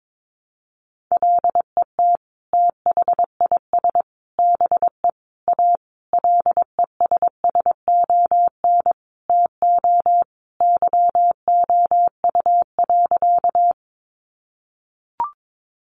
Morse code, heard as a keyed tone, about 22 wpm, 700 Hz. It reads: LET THIS BE A LESSON TO YOU.
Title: Sentences from Top 1200 Words 22wpm